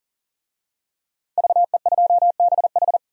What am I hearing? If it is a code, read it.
Morse code, heard as a keyed tone, about 40 wpm, 700 Hz. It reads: VE2BH
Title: DXCC-100 Call Signs - 5 Characters - 40wpm